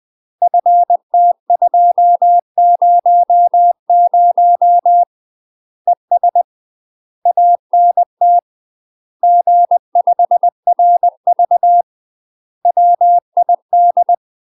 Morse code, heard as a keyed tone, about 20 wpm, 700 Hz. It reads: FT200 ES ANT G5RV WID
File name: UK MASTER QSOs TRAINING / 20WPM / UK MASTER QSOs 0028